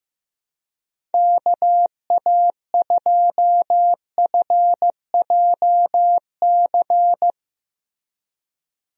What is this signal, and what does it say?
Morse code, heard as a keyed tone, about 16 wpm, 700 Hz. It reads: KA2FJC
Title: USA Call Signs - 6 Characters - Repeated 2x - 15wpm